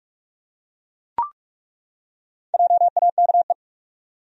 Morse code, heard as a keyed tone, about 42 wpm, 700 Hz. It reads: JAKE
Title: USA Common Names - Max 4 Letters - 45wpm